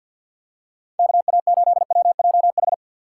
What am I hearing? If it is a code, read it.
Morse code, heard as a keyed tone, about 53 wpm, 700 Hz. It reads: K A 8 W J H
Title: USA Call Signs - 6 Characters - 8x Character Spacing - 50wpm